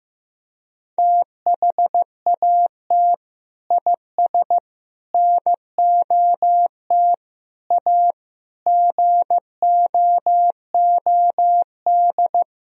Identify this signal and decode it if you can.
Morse code, heard as a keyed tone, about 15 wpm, 700 Hz. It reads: THAT IS NOT A GOOD
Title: Sentences from Top 1000 Words - Review Part 3 - 15wpm